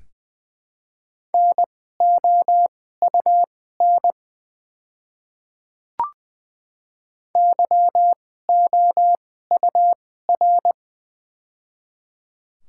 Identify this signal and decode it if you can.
Morse code, heard as a keyed tone, about 21 wpm, spaced as for 15 wpm, 700 Hz. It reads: NOUN YOUR
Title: 2x Character Spacing - Top 500 Words - 20wpm